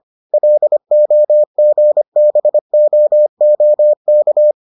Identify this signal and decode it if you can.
Morse code, heard as a keyed tone, about 25 wpm, 600 Hz. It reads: LOGBOOK